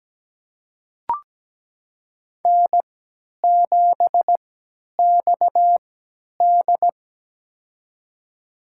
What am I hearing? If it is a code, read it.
Morse code, heard as a keyed tone, about 17 wpm, spaced as for 10 wpm, 700 Hz. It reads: N7XD
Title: USA Call Signs - 4 Characters - 3x Character Spacing - 17wpm